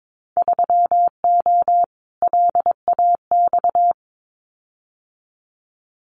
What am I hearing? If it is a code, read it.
Morse code, heard as a keyed tone, about 22 wpm, 700 Hz. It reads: 3O LAX
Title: Winter Field Day Exchanges Encore 22wpm